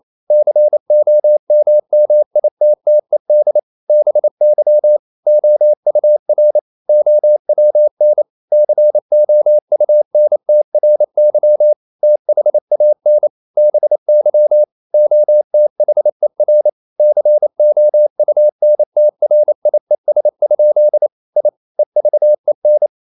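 Morse code, heard as a keyed tone, about 28 wpm, 600 Hz. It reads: COMMITTED BY OUR OWN COUNTRY THAN BY OTHER COUNTRIES? I EVEN
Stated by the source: Morse code practice at 28 wpm, text of Ken Follett's novel Never in English, from CW med Gnister.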